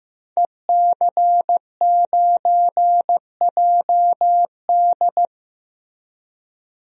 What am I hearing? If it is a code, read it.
Morse code, heard as a keyed tone, about 15 wpm, 700 Hz. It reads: EC9JD
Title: DXCC-100 Call Signs - 5 Characters - 15wpm